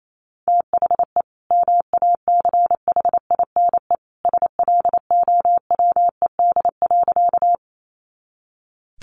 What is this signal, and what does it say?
Morse code, heard as a keyed tone, about 28 wpm, 700 Hz. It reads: THE MACHINE SLOWED.
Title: A Sound Of Thunder 28wpm